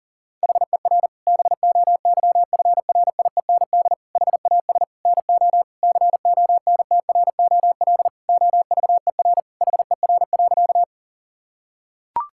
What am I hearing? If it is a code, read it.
Morse code, heard as a keyed tone, about 40 wpm, 700 Hz. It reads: HER BOYFRIEND HAS NO CONTROL OVER HER.
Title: Sentences from Top 1000 Words - Review Part 5 - 40wpm